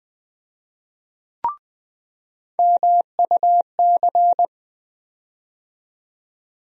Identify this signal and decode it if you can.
Morse code, heard as a keyed tone, about 20 wpm, 700 Hz. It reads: MUC
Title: USA Call Sign Postfixes - 3 Letters 20wpm